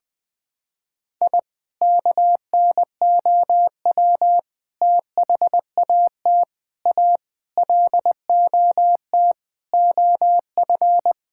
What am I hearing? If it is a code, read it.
Morse code, heard as a keyed tone, about 20 wpm, 700 Hz. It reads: I KNOW THAT A LOT OF
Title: Sentences with 4-Word Phrases - 20wpm